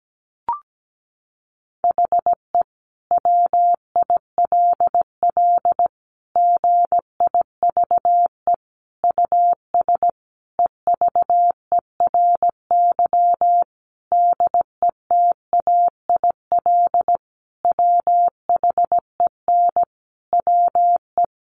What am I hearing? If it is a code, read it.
Morse code, heard as a keyed tone, about 17 wpm, 700 Hz. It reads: HE WILL GIVE US EVERY DETAIL WHEN WE
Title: Sentences from Top 1500 Words 17wpm